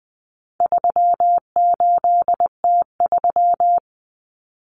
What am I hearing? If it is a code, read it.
Morse code, heard as a keyed tone, about 20 wpm, 700 Hz. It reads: 38T3
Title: Common Cut Numbers - 4-Digit 20wpm